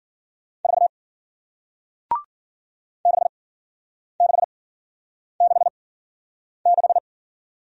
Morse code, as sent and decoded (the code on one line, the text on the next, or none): ....- / -.... -.... -.... -....
4 6666